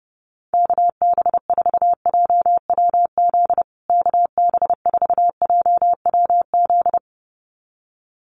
Morse code, heard as a keyed tone, about 30 wpm, 700 Hz. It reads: KB4JWZ KB4JWZ
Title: USA Call Signs Repeated 2x 30wpm